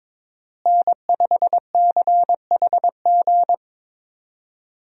Morse code, heard as a keyed tone, about 22 wpm, 700 Hz. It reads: N5CHG
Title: USA Call Signs - 5 Characters - Repeated 2x - 22wpm